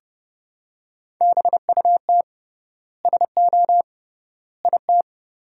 Morse code, heard as a keyed tone, about 30 wpm, 700 Hz. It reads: BUT SO IT
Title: 3x Word-Spacing - Sentences from Top 100 Words - 30wpm